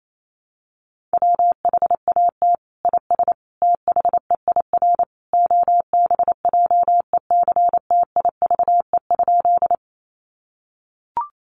Morse code, heard as a keyed tone, about 28 wpm, 700 Hz. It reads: WHAT IS THEIR OBJECTIVE?
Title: Sentences from Top 1200 Words 28wpm